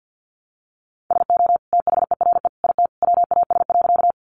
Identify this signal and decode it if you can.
Morse code, heard as a keyed tone, about 50 wpm, 700 Hz. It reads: SO THERE IT WAS.
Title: Sentences from Top 100 Words 50wpm